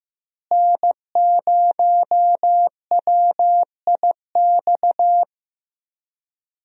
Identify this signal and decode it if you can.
Morse code, heard as a keyed tone, about 15 wpm, 700 Hz. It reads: N0WIX
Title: USA Call Signs - 5 Characters - Repeated 3x - 15wpm